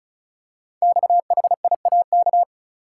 Morse code, heard as a keyed tone, about 36 wpm, 700 Hz. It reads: XHIAK
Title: Code Groups - 5 Letter:Numbers - 35wpm